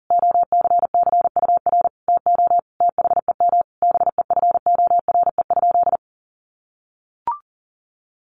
Morse code, heard as a keyed tone, about 37 wpm, 700 Hz. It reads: OCCUR TO THEM BEFORE?
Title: Sentences from Top 900 Words 40wpm